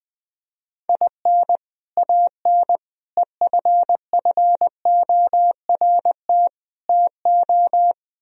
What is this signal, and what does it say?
Morse code, heard as a keyed tone, about 20 wpm, 700 Hz. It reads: IN AN EFFORT TO
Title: Speed-Racing - Sets of 4 Words 17wpm